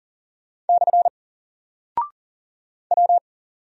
Morse code, heard as a keyed tone, about 40 wpm, 700 Hz. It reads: / W
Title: Single Character (Letter:Number:Punctuation:Prosign) Alt 40wpm